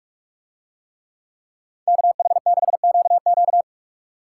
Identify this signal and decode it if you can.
Morse code, heard as a keyed tone, about 46 wpm, 700 Hz. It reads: KH6QQ